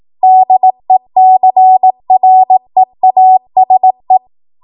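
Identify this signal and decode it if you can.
Morse code, heard as a keyed tone, about 18 wpm, 750 Hz. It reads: DECREASE
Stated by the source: ARRL 18 wpm Morse code practice archive transmission.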